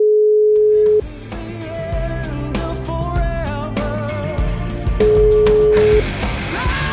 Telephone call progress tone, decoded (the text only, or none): ringback tone